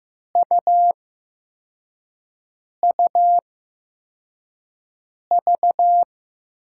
Morse code, heard as a keyed tone, about 15 wpm, spaced as for 4 wpm, 700 Hz. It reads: UUV
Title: USA Call Signs - 5 Characters - 8x Character Spacing - 15wpm